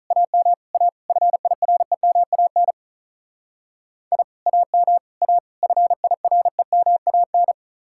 Morse code, heard as a keyed tone, about 44 wpm, 700 Hz. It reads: AM A FIREMAN I AM A FIREMAN